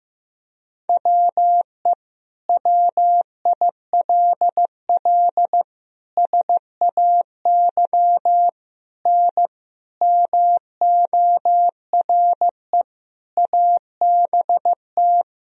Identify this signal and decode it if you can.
Morse code, heard as a keyed tone, about 15 wpm, 700 Hz. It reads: WE WILL SAY N MORE ABT
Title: Sentences from Top 200 Words and Abbreviations 15wpm